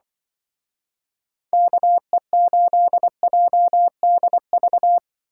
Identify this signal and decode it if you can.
Morse code, heard as a keyed tone, about 24 wpm, 700 Hz. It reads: KE8JDV
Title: Speed Racing - USA 2x3 Call Signs 20wpm